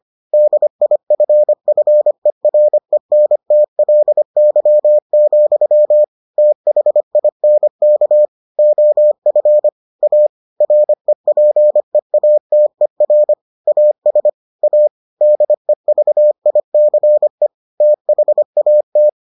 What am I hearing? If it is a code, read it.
Morse code, heard as a keyed tone, about 25 wpm, 600 Hz. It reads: DIFFERENTLY, THINK OF A REPEATER AS A DEVICE THAT